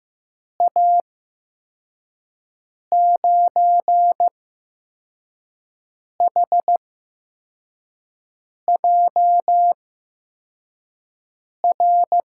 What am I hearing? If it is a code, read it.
Morse code, heard as a keyed tone, about 15 wpm, spaced as for 4 wpm, 700 Hz. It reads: A9HJR
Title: USA Call Signs - 6 Characters - 8x Character Spacing - 15wpm